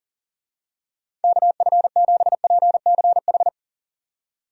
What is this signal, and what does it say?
Morse code, heard as a keyed tone, about 40 wpm, 700 Hz. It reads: KF7PCH
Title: USA Call Signs - 6 Characters - 40wpm